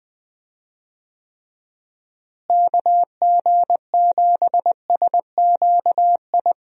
Morse code, heard as a keyed tone, about 20 wpm, 700 Hz. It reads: KG7SQI
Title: USA Call Signs - 6 Characters - Repeated 3x - 20wpm